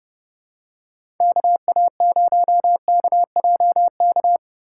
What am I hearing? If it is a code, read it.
Morse code, heard as a keyed tone, about 30 wpm, 700 Hz. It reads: KA0KJK
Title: USA Call Signs - 6 Characters - 4x Character Spacing - 30wpm